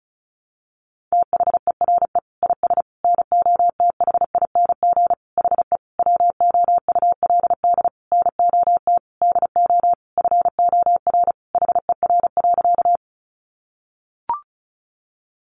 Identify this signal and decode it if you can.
Morse code, heard as a keyed tone, about 34 wpm, 700 Hz. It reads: THERE IS NOTHING HE WOULD NOT DO FOR HER.
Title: Sentences from Top 600 Words 35wpm